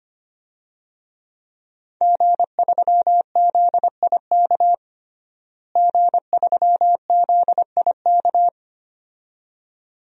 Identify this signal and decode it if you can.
Morse code, heard as a keyed tone, about 25 wpm, 700 Hz. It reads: G3ZIK G3ZIK
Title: DXCC-100 Call Signs - 5 Characters - Repeated 2x - 25wpm